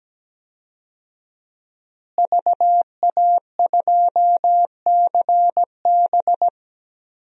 Morse code, heard as a keyed tone, about 17 wpm, 700 Hz. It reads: VA2CB
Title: DXCC-100 Call Signs - 5 Characters - Repeated 2x - 17wpm